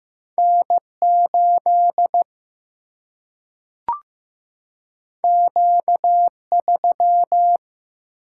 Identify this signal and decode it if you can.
Morse code, heard as a keyed tone, about 15 wpm, 700 Hz. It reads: N 8 Q 3
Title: Single-Letter Single-Number 15wpm